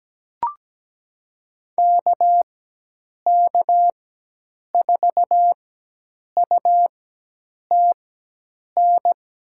Morse code, heard as a keyed tone, about 17 wpm, spaced as for 8 wpm, 700 Hz. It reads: KK4UTN
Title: USA Call Signs - 4x Character Spacing - 17wpm